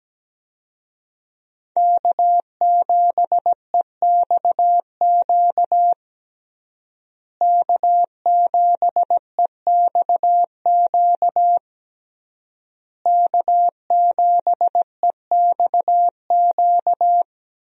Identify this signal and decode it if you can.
Morse code, heard as a keyed tone, about 17 wpm, 700 Hz. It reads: K7EXQ K7EXQ K7EXQ
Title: USA Call Signs - 5 Characters - Repeated 3x - 17wpm